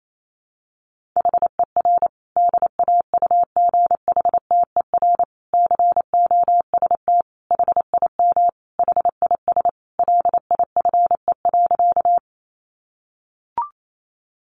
Morse code, heard as a keyed tone, about 28 wpm, 700 Hz. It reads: HER DAUGHTER COST HIM HIS LIFE.
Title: Sentences from Top 1500 Words 28wpm